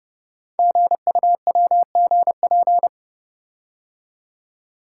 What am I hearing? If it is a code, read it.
Morse code, heard as a keyed tone, about 29 wpm, 690 Hz. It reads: GUWGP
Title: Code Groups - 5 Letters - 30wpm